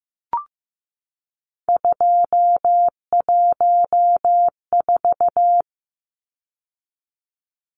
Morse code, heard as a keyed tone, about 15 wpm, 700 Hz. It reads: 214
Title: Numbers - 3-Digit 15wpm